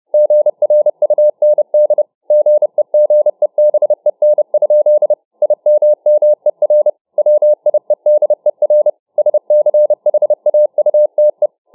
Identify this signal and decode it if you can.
Morse code, heard as a keyed tone, about 29 wpm, 600 Hz. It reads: GRUND GEGEBEN? IMMER WIEDER SCHAUTE